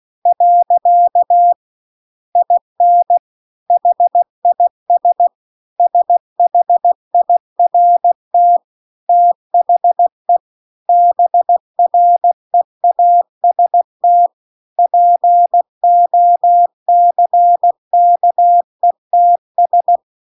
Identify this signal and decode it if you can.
Morse code, heard as a keyed tone, about 16 wpm, 700 Hz. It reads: . IN HIS SHIRT THE BREAST POCKETS
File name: CW-The-Big-Twoheaded-River-2-16WPM